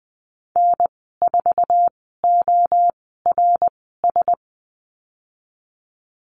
Morse code, heard as a keyed tone, about 20 wpm, spaced as for 15 wpm, 700 Hz. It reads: N4ORS
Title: 082 Lesson 14g - Number 2 - Call Signs 20wpm